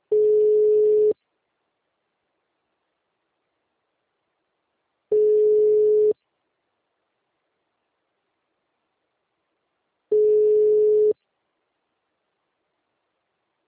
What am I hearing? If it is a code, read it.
Ringback tone (Europe, 425 Hz).